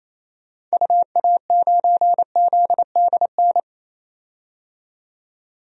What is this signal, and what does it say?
Morse code, heard as a keyed tone, about 28 wpm, 700 Hz. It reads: UA9ZDN
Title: DXCC-100 Call Signs - 6 Characters - 28wpm